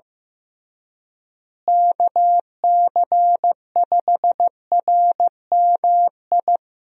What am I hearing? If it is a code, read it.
Morse code, heard as a keyed tone, about 15 wpm, 700 Hz. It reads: KC5RMI
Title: USA Call Signs - 6 Characters - Repeated 3x - 15wpm